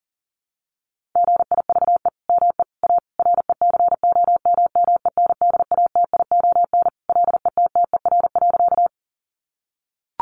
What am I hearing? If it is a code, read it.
Morse code, heard as a keyed tone, about 37 wpm, 700 Hz. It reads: GIVE ME A RECOMMENDATION LETTER.
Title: Sentences from Top 1400 Words 40wpm